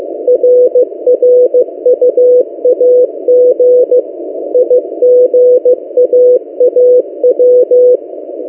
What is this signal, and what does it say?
Morse code, heard as a keyed tone, about 15 wpm, 500 Hz. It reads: RRUAG IGAAW
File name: M08a- -1355utc-8096khz